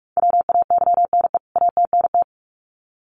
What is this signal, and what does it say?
Morse code, heard as a keyed tone, about 44 wpm, 700 Hz. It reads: WAYNE ATNT
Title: CWT Exchange 45wpm